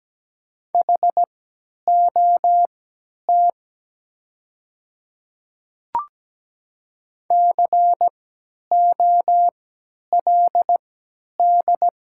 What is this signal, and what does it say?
Morse code, heard as a keyed tone, about 17 wpm, spaced as for 10 wpm, 700 Hz. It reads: HOT COLD
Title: 3x Character Spacing - Top 500 Words - 17wpm